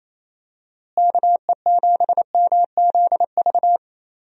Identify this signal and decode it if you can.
Morse code, heard as a keyed tone, about 27 wpm, 700 Hz. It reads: KE7MZV
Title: USA Call Signs - 6 Characters - Repeated 2x - 28wpm